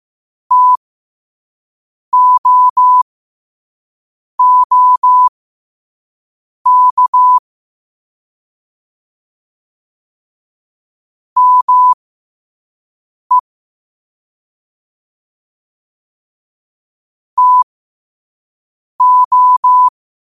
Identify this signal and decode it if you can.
Morse code, heard as a keyed tone, about 15 wpm, spaced as for 5 wpm, 1000 Hz. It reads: TOOK ME TO
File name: qotdfiles / 5wpm